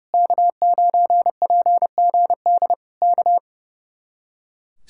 Morse code, heard as a keyed tone, about 30 wpm, 700 Hz. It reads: K9PGD K
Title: Speed-Racing Common QSO Phrases - 30wpm